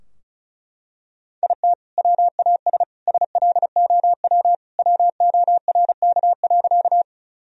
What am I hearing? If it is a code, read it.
Morse code, heard as a keyed tone, about 34 wpm, 700 Hz. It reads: IT WAS SLOW WORK.
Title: Sentences from Top 1000 Words - Review Part 2 - 35wpm